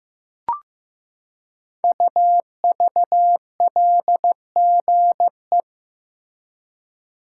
Morse code, heard as a keyed tone, about 15 wpm, 700 Hz. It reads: UVLGE
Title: Code Groups - 5 Letter:Numbers - 15wpm